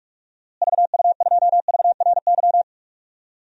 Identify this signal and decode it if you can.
Morse code, heard as a keyed tone, about 42 wpm, 700 Hz. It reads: VU2VRY
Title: DXCC-100 Call Signs - 6 Characters - Repeated 2x - 45wpm